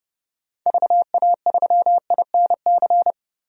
Morse code, heard as a keyed tone, about 30 wpm, 690 Hz. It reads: VA3INC
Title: DXCC-100 Call Signs - 6 Characters - 30wpm